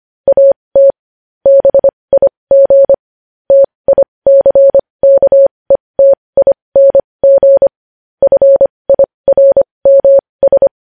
Morse code, heard as a keyed tone, about 25 wpm, spaced as for 20 wpm, 550 Hz. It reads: AT BIG TICKETING FIRMS
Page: news-headlines-in-morse-code-at-20-wpm